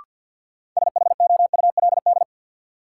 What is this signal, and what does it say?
Morse code, heard as a keyed tone, about 53 wpm, 700 Hz. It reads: SHOULD
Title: Sentences from Top 300 Words - New Word Drill - 50wpm